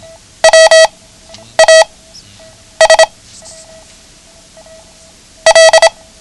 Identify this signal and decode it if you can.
Morse code, heard as a keyed tone, about 26 wpm, 650 Hz. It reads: W A S L
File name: ham radio / CodeQuick / Code Quick 09